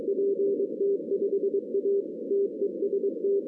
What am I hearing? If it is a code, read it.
Morse code, heard as a keyed tone, about 21 wpm, 410 Hz. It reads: PT5A TEST
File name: 2016 arrl dx cw / 2016-ARRL-DX-CW-K5TR- - 0-31